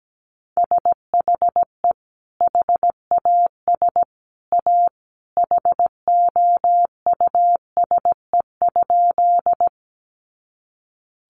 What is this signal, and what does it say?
Morse code, heard as a keyed tone, about 17 wpm, 700 Hz. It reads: SHE HAS A HOUSE?
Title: Sentences from Top 200 Words 17wpm